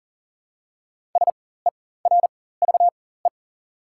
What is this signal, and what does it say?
Morse code, heard as a keyed tone, about 40 wpm, spaced as for 19 wpm, 690 Hz. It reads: SERVE
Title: 4x Character Spacing - Top 500 Words - 40wpm